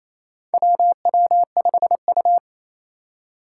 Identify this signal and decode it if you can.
Morse code, heard as a keyed tone, about 27 wpm, 700 Hz. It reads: WW5U
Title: USA Call Signs - 2x Character Spacing - 28wpm